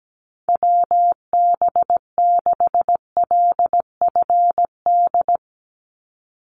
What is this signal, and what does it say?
Morse code, heard as a keyed tone, about 17 wpm, 700 Hz. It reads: WB6LFD
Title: USA Call Signs 17wpm